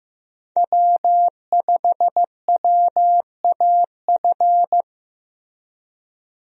Morse code, heard as a keyed tone, about 15 wpm, 700 Hz. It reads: W5WAF
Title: USA Call Signs - 5 Characters - Repeated 3x - 15wpm